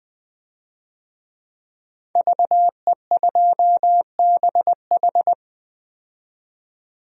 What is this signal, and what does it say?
Morse code, heard as a keyed tone, about 20 wpm, 700 Hz. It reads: VE2BH
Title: DXCC-100 Call Signs - 5 Characters - 20wpm